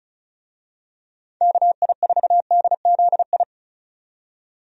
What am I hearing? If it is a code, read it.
Morse code, heard as a keyed tone, about 36 wpm, 700 Hz. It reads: KI4DZI
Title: USA Call Signs - 6 Characters - 35wpm